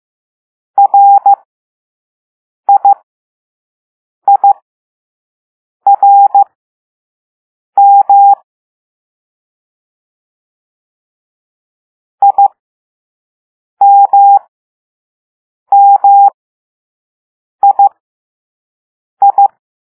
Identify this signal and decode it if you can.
Morse code, heard as a keyed tone, about 15 wpm, spaced as for 5 wpm, 800 Hz. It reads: RIIRM IMMII
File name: Morse / Level2B